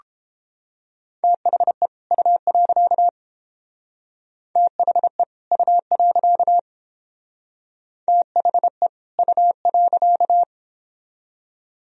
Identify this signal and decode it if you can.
Morse code, heard as a keyed tone, about 30 wpm, 700 Hz. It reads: THE U. THE U. THE U.